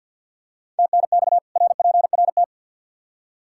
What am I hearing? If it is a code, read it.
Morse code, heard as a keyed tone, about 53 wpm, 700 Hz. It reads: TNX RPRT
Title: Practice QSO - Exchange 2 - Info - 50wpm